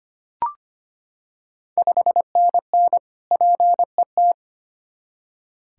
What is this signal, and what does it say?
Morse code, heard as a keyed tone, about 25 wpm, 690 Hz. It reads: 5NN PET
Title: Ontario QSO Party Exchange 25wpm